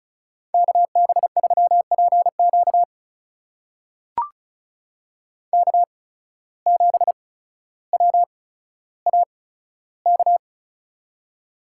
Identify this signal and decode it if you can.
Morse code, heard as a keyed tone, about 36 wpm, spaced as for 15 wpm, 700 Hz. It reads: KB3PQ K7WAK
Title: USA Call Signs - 5 Characters - 8x Character Spacing - 35wpm